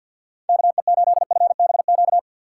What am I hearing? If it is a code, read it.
Morse code, heard as a keyed tone, about 51 wpm, 700 Hz. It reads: KE8FBQ
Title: USA Call Signs - 6x Character Spacing - 50wpm